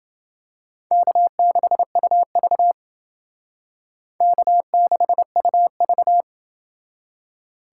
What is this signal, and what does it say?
Morse code, heard as a keyed tone, about 29 wpm, 700 Hz. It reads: K6UV K6UV